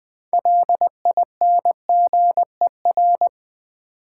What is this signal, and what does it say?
Morse code, heard as a keyed tone, about 20 wpm, 700 Hz. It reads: LINGER
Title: 6 Letter Words - First 3 Letters Given - 20wpm